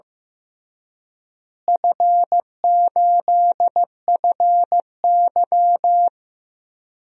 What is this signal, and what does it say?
Morse code, heard as a keyed tone, about 15 wpm, 700 Hz. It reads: F8FY
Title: DXCC-100 Call Signs - 4 Characters - Repeated 3x - 15wpm